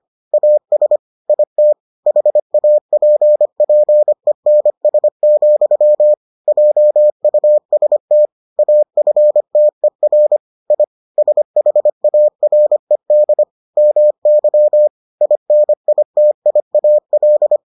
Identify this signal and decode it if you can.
Morse code, heard as a keyed tone, about 25 wpm, 600 Hz. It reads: AS IT HAPPENS, JUST AFTER I SHARED MY INITIAL